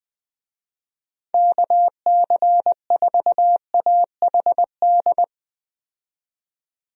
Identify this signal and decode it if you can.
Morse code, heard as a keyed tone, about 20 wpm, 700 Hz. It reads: KC4AHD
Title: USA Call Signs - 6 Characters - 20wpm